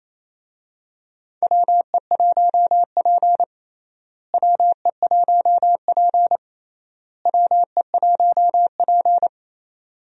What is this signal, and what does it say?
Morse code, heard as a keyed tone, about 27 wpm, 700 Hz. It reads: WE1P WE1P WE1P
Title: USA Call Signs - 4 Characters - Repeated 3x - 28wpm